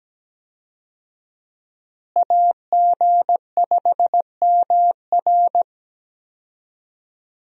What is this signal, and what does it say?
Morse code, heard as a keyed tone, about 17 wpm, 700 Hz. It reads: AG5MR
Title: USA Call Signs - 5 Characters - Repeated 3x - 17wpm